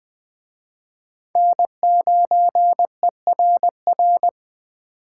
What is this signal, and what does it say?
Morse code, heard as a keyed tone, about 21 wpm, 700 Hz. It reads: N9ERR
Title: Speed Racing - USA Call Signs 17wpm